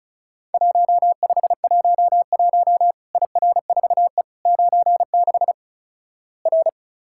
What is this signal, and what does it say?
Morse code, heard as a keyed tone, about 36 wpm, 700 Hz. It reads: 1511 IR4E 96 R
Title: Worked All Europe DX Contest - QTC Exchange - 35wpm